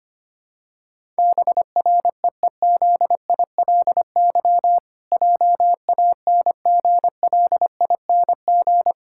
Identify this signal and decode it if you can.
Morse code, heard as a keyed tone, about 25 wpm, 700 Hz. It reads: BREEZILY JANGLING
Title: Pangram Challenge v2 - 25wpm